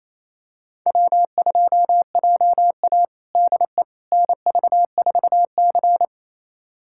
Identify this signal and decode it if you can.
Morse code, heard as a keyed tone, about 27 wpm, 700 Hz. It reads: W2JA DE NV4C
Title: USA Callsign DE Callsign - 4 Characters - 28wpm